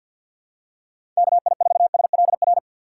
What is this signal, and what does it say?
Morse code, heard as a keyed tone, about 48 wpm, 700 Hz. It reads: KI4SLR
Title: USA Call Signs - 6 Characters - 4x Character Spacing - 50wpm